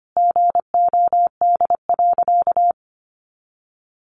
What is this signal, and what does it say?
Morse code, heard as a keyed tone, about 25 wpm, 700 Hz. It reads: GOD.
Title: A Sound Of Thunder 25wpm